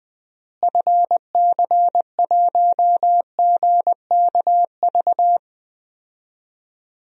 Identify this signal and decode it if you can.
Morse code, heard as a keyed tone, about 20 wpm, 700 Hz. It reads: FC1GKV
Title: DXCC-100 Call Signs - 6 Characters - Repeated 2x - 20wpm